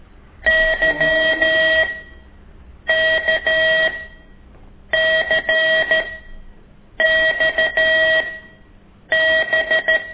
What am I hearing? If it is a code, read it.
Morse code, heard as a keyed tone, about 13 wpm, spaced as for 7 wpm, 1900 Hz, with noise. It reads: YKCXB
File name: ik1weg2 / CORSI CW